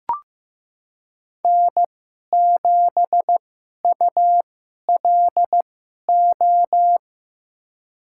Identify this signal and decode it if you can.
Morse code, heard as a keyed tone, about 15 wpm, spaced as for 11 wpm, 700 Hz. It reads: N7ULO
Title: USA Call Signs - 5 Characters - 2x Character Spacing - 15wpm